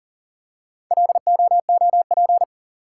Morse code, heard as a keyed tone, about 39 wpm, 700 Hz. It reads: LOOP
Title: Top 180 QSO Elements 40wpm